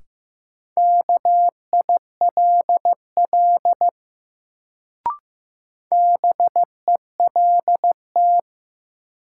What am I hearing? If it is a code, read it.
Morse code, heard as a keyed tone, about 15 wpm, 700 Hz. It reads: KILL BELT
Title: Sentences from Top 1400 Words - New Word Drill Alt - 15wpm